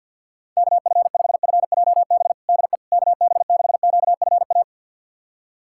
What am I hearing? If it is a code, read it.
Morse code, heard as a keyed tone, about 51 wpm, 700 Hz. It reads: KF5FJB DE KB6QFA
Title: USA Callsign DE Callsign - 6 Characters - 50wpm